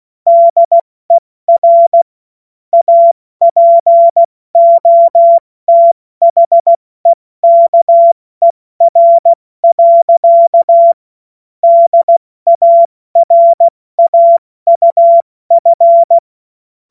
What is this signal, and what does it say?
Morse code, heard as a keyed tone, about 16 wpm, 680 Hz. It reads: DER APOTHEKER. DARAUF
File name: Morsekurs / Witze / Witz 25. Mai 2021, 70 F80